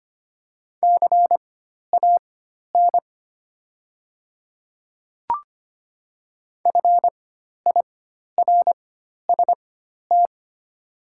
Morse code, heard as a keyed tone, about 25 wpm, spaced as for 12 wpm, 700 Hz. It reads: CAN FIRST
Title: 4x Character Spacing - Top 100 Words - 25wpm